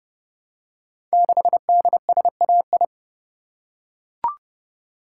Morse code, heard as a keyed tone, about 30 wpm, 700 Hz. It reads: 6DSAI
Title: Code Groups - 5 Letter:Numbers - 30wpm